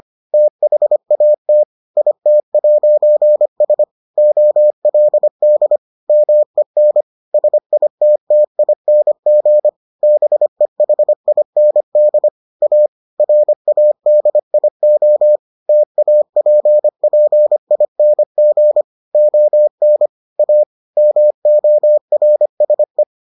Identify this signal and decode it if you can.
Morse code, heard as a keyed tone, about 25 wpm, 600 Hz. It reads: THAT IT'S OLD MEN SITTING BEHIND A RADIO TAPPING ON A MORSE